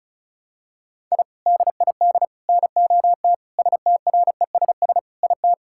Morse code, heard as a keyed tone, about 36 wpm, 700 Hz. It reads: I DID NOT STRESS IT
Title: Sentences from Top 1000 Words - Review Part 6 - 35wpm